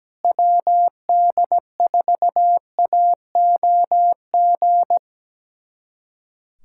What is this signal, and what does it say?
Morse code, heard as a keyed tone, about 17 wpm, 700 Hz. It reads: WD4AOG
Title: USA Callsign DE Callsign 17wpm